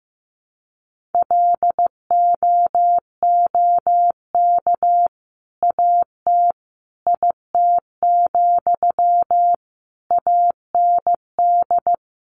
Morse code, heard as a keyed tone, about 15 wpm, 700 Hz. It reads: LOOK AT IT, AND
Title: Sentences from Top 200 Words 15wpm